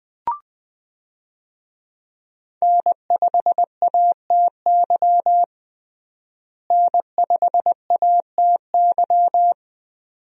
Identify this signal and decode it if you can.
Morse code, heard as a keyed tone, about 20 wpm, 700 Hz. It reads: N5ATY N5ATY
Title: USA Call Signs - 5 Characters - Repeated 2x - 20wpm